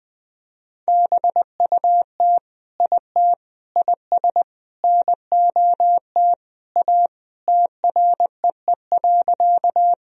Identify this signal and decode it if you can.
Morse code, heard as a keyed tone, about 21 wpm, 700 Hz. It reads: BUT IT IS NOT A TREE.
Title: Sentences from Top 1000 Words - Review Part 1 - 20wpm